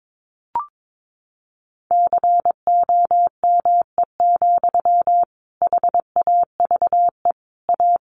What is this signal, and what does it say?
Morse code, heard as a keyed tone, about 22 wpm, 700 Hz. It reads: COME, HAVE A
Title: Sentences from Top 1500 Words 22wpm